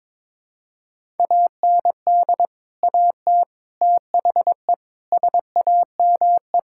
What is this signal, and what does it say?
Morse code, heard as a keyed tone, about 22 wpm, 700 Hz. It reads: AND AT THE SAME
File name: practice / 22wpm / Build A Phrase - 5-Word Phrases - 22wpm